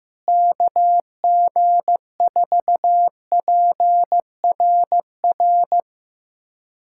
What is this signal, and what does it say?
Morse code, heard as a keyed tone, about 16 wpm, 700 Hz. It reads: KG4PRR
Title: USA Callsign DE Callsign - 6 Characters - 15wpm